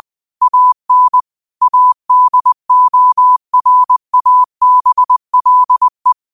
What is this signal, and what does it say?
Morse code, heard as a keyed tone, about 20 wpm, 1000 Hz. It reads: AN ADORABLE
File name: qotdfiles / 20wpm